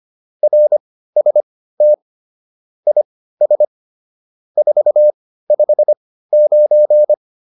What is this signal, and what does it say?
Morse code, heard as a keyed tone, about 25 wpm, spaced as for 15 wpm, 600 Hz. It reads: RST IS 459